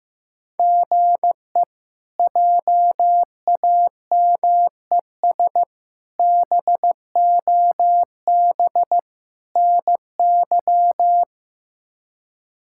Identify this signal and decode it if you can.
Morse code, heard as a keyed tone, about 15 wpm, 700 Hz. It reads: GE JAMES BOB NY
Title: K1USN SST QSO Practice - 15wpm